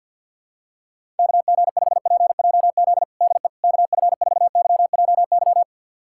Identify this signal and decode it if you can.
Morse code, heard as a keyed tone, about 50 wpm, 700 Hz. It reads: KG5PJZ DE KF4YJY
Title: USA Callsign DE Callsign - 6 Characters - 50wpm